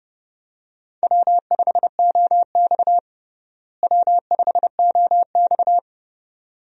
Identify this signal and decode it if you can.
Morse code, heard as a keyed tone, about 30 wpm, 700 Hz. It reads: W5OX W5OX
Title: USA Call Signs - 4 Characters - Repeated 2x - 30wpm